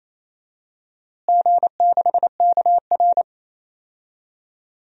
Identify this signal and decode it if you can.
Morse code, heard as a keyed tone, about 27 wpm, 700 Hz. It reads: G6KR
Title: DXCC-100 Call Signs - 4 Characters - Repeated 2x - 28wpm